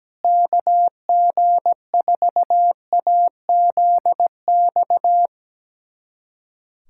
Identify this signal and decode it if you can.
Morse code, heard as a keyed tone, about 17 wpm, 700 Hz. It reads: KG4AZX
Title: USA Callsign DE Callsign 17wpm